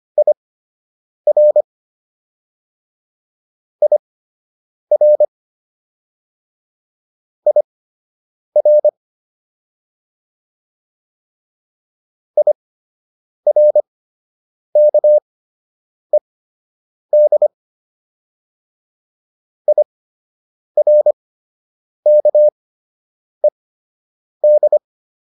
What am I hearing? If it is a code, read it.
Morse code, heard as a keyed tone, about 25 wpm, spaced as for 8 wpm, 600 Hz. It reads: IR IR IR IRKED IRKED